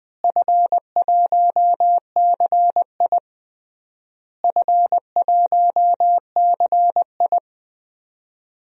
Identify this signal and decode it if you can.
Morse code, heard as a keyed tone, about 20 wpm, 700 Hz. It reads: F1CI F1CI
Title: DXCC-100 Call Signs - 4 Characters - Repeated 3x - 20wpm